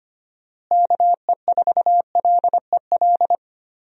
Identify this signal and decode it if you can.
Morse code, heard as a keyed tone, about 25 wpm, 700 Hz. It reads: KE4LEL
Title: USA Call Signs - 6 Characters - Repeated 3x - 25wpm